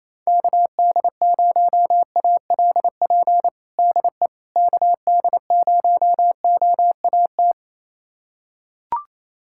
Morse code, heard as a keyed tone, about 28 wpm, 700 Hz. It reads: KD0ALP DE KD0OAT
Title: USA Callsign DE Callsign 28wpm